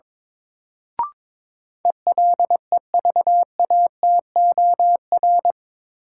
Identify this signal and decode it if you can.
Morse code, heard as a keyed tone, about 22 wpm, 700 Hz. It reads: ELEVATOR
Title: Sentences from Top 1600 Words - New Word Drill Alt - 22wpm